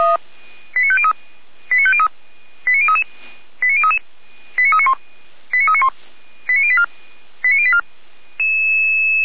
Five-tone selective call (ZVEI), heard as quick runs of five tones, tones of about 70 ms. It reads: 89503 89503 89033 89033 89391 89391 89084 89084
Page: audio.f85360.feuerwehr-orlamuende.org